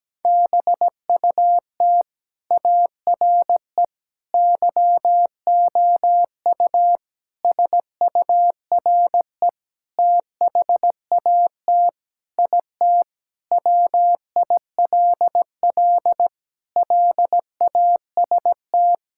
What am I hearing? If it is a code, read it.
Morse code, heard as a keyed tone, about 17 wpm, 700 Hz. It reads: BUT ARE YOU SURE THAT IT WILL LAST
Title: Sentences from Top 1600 Words 17wpm